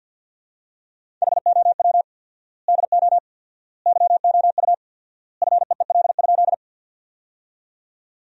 Morse code, heard as a keyed tone, about 50 wpm, 700 Hz. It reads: HOW DO YOU FEEL?
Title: 4x Word-Spacing - Sentences from Top 400 Words - 50wpm